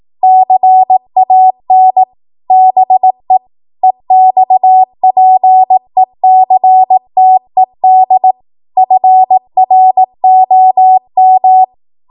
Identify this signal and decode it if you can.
Morse code, heard as a keyed tone, about 18 wpm, 750 Hz. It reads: CAN BE EXPECTED FROM
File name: Morse / 18WPM